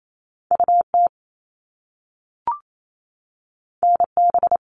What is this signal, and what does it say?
Morse code, heard as a keyed tone, about 28 wpm, 700 Hz. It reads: UT NB
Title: All Cut Numbers - 2-Digit 28wpm